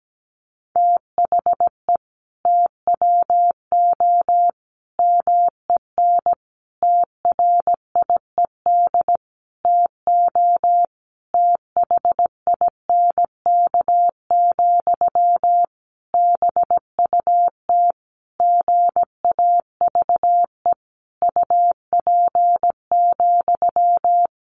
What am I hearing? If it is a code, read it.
Morse code, heard as a keyed tone, about 17 wpm, 700 Hz. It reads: THE TWO MEN TRIED TO THINK, BUT GAVE UP,
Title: A Sound Of Thunder 17wpm